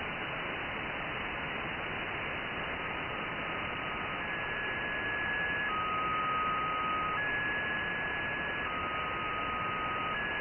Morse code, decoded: none